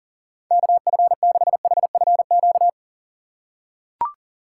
Morse code, heard as a keyed tone, about 40 wpm, 700 Hz. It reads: KF6HFQ
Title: USA Call Signs - 6 Characters - 4x Character Spacing - 40wpm